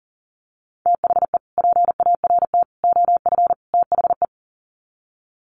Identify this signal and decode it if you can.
Morse code, heard as a keyed tone, about 37 wpm, 700 Hz. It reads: THE PART OF THE
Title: Sets of 4 Words 40wpm